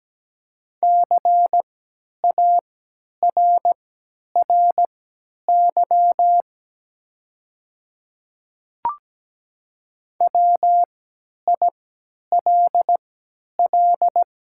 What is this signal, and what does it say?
Morse code, heard as a keyed tone, about 17 wpm, spaced as for 10 wpm, 700 Hz. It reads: CARRY WILL
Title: 3x Character Spacing - Top 300 Words - 17wpm